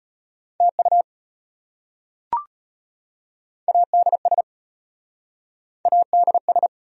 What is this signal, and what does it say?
Morse code, heard as a keyed tone, about 34 wpm, 700 Hz. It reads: TU ADS ADS